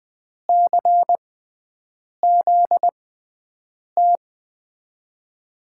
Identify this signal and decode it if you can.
Morse code, heard as a keyed tone, about 20 wpm, spaced as for 7 wpm, 700 Hz. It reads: CZT
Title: USA Call Signs - 6 Characters - 6x Character Spacing - 20wpm